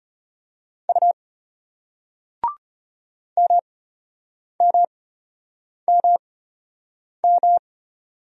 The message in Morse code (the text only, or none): U MMMM